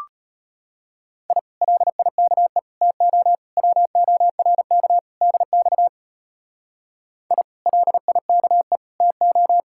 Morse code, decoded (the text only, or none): I LIKE TO WORK DX I LIKE TO